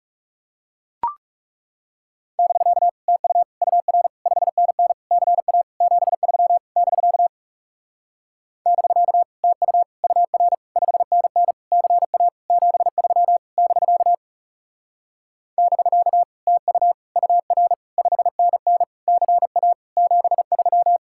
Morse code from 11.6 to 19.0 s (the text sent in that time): CA 73 <BK> <BK> TU UR 5NN